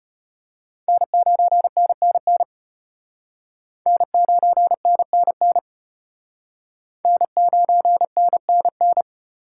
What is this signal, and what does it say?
Morse code, heard as a keyed tone, about 33 wpm, 700 Hz. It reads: N9NNN N9NNN N9NNN